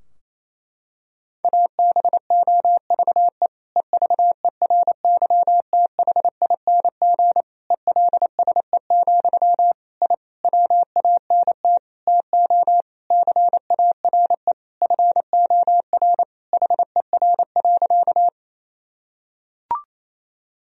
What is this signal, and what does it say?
Morse code, heard as a keyed tone, about 27 wpm, 700 Hz. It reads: ABOVE EVERYTHING ELSE, I WANT TO CARE FOR HER.
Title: Sentences from Top 1000 Words - Review Part 2 - 28wpm